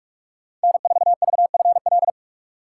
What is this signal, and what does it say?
Morse code, heard as a keyed tone, about 46 wpm, 700 Hz. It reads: N4VFL
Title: USA Call Signs - 3x Character Spacing - 45wpm